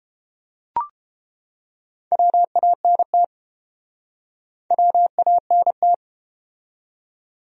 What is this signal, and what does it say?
Morse code, heard as a keyed tone, about 30 wpm, 700 Hz. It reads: WANT WANT